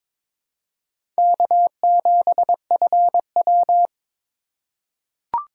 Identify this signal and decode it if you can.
Morse code, heard as a keyed tone, about 21 wpm, 700 Hz. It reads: K7FW
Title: USA Call Signs - 4 Characters - 3x Character Spacing - 22wpm